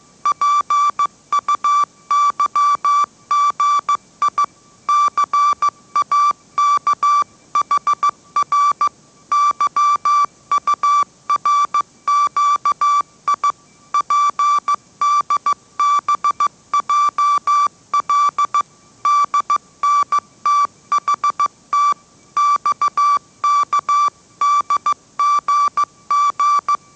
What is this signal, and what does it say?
Morse code, heard as a keyed tone, about 15 wpm, 1160 Hz. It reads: PUYGI CAKHR YURQI PDBJL DNTHT XKDGG
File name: mrx / mrx065 16wpm